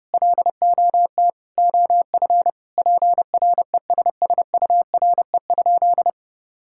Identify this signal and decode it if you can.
Morse code, heard as a keyed tone, about 32 wpm, 700 Hz. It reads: LOT OF PRESSURE?
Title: Sentences from Top 1000 Words - Review Part 6 - 30wpm